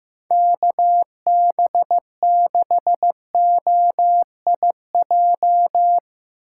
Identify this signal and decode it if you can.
Morse code, heard as a keyed tone, about 15 wpm, 700 Hz. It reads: KB6OIJ
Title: USA Call Signs - 6 Characters - 4x Character Spacing - 15wpm